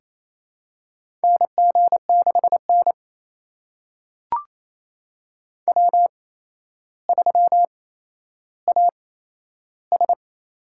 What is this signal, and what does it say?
Morse code, heard as a keyed tone, about 28 wpm, spaced as for 12 wpm, 700 Hz. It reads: NG6N W3AS
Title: USA Call Signs - 4 Characters - 8x Character Spacing - 28wpm